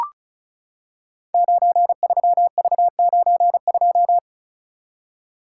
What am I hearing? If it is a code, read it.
Morse code, heard as a keyed tone, about 36 wpm, 700 Hz. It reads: 93V92
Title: Code Groups - 5 Letter:Numbers - 35wpm